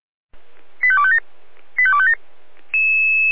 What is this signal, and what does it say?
Five-tone selective call (ZVEI), heard as quick runs of five tones, tones of about 70 ms. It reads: 85257 85257
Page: /496/SDR/ZVEI/85252/2022.05.15/